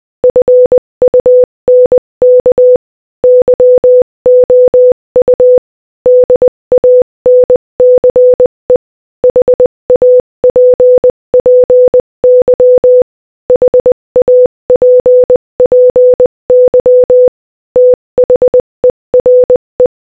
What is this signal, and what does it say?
Morse code, heard as a keyed tone, about 20 wpm, 500 Hz. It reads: FUNK YOU DANCE HAPPY HAPPY THERE
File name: Morse-Code-Audio-Speed-20-1